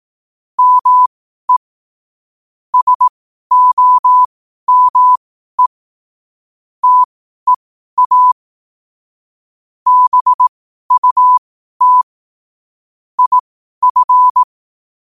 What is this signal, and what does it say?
Morse code, heard as a keyed tone, about 18 wpm, spaced as for 12 wpm, 1000 Hz. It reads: ME SOME TEA BUT IF
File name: qotdfiles / 12wpm